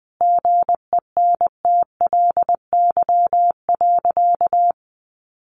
Morse code, heard as a keyed tone, about 20 wpm, 700 Hz. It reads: GENTLY.
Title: Sentences from Top 1500 Words 20wpm